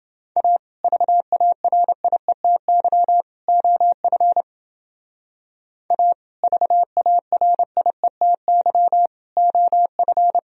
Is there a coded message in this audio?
Yes: Morse code from 0.4 to 10.4 s, heard as a keyed tone, about 29 wpm, 700 Hz. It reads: A VARIETY OF A VARIETY OF